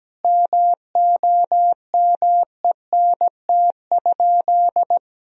Morse code, heard as a keyed tone, about 17 wpm, 700 Hz. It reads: MOMENT?
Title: Sentences from Top 1000 Words - Review Part 1 - 17wpm